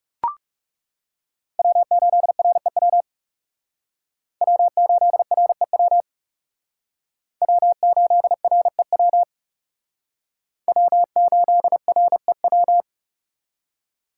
Morse code, heard as a keyed tone, about 37 wpm, 700 Hz. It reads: W8REW W8REW W8REW W8REW
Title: Speed Racing - USA 1x3 Call Signs 30wpm